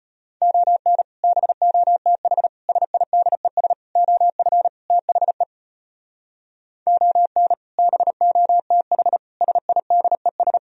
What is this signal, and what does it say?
Morse code, heard as a keyed tone, about 33 wpm, 700 Hz. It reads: ON BOTH SIDES OF THE ON BOTH SIDES